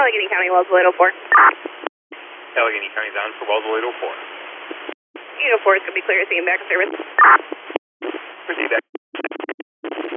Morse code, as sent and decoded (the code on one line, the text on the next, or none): none